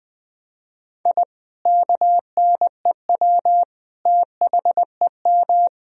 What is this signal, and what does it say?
Morse code, heard as a keyed tone, about 21 wpm, 700 Hz. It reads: I KNEW THEM
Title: Sentences from Top 1000 Words - Review Part 1 - 20wpm